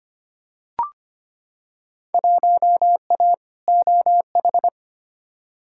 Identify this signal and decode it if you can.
Morse code, heard as a keyed tone, about 25 wpm, 700 Hz. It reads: 1A OH
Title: Field Day Exchanges - 25wpm